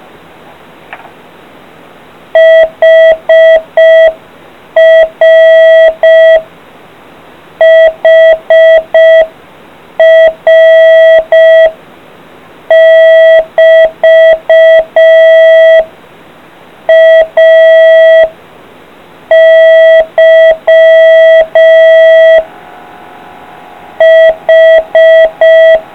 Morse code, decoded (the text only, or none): HR HR <BT> AY H